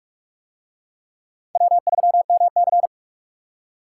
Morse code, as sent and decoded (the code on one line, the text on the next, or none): .-- ...-- -- -.-.
W3MC